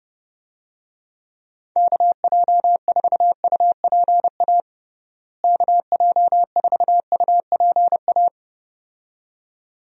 Morse code, heard as a keyed tone, about 31 wpm, 700 Hz. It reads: KJ4UPA KJ4UPA
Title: USA Call Signs - 6 Characters - Repeated 2x - 30wpm